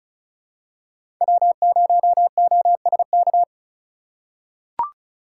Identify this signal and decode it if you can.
Morse code, heard as a keyed tone, about 34 wpm, 700 Hz. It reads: W0OSK
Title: USA Call Signs - 5 Characters - 2x Character Spacing - 35wpm